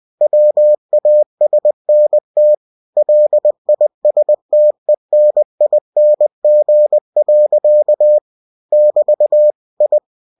Morse code, heard as a keyed tone, about 20 wpm, 600 Hz. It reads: WASNT LISTENING. <BT> I
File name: cw-books / winnie-the-pooh / winnie-20wpm-page-0085